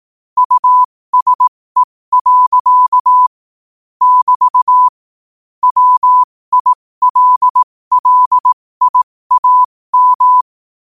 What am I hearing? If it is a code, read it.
Morse code, heard as a keyed tone, about 18 wpm, spaced as for 15 wpm, 1000 Hz. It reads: USE. <BT> WILLIAM
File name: qotdfiles / 15wpm